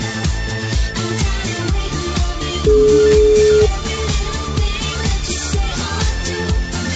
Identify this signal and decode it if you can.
Ringback tone (Europe, 425 Hz), with noise.